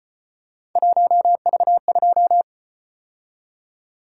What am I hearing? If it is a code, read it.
Morse code, heard as a keyed tone, about 33 wpm, 700 Hz. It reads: 1V2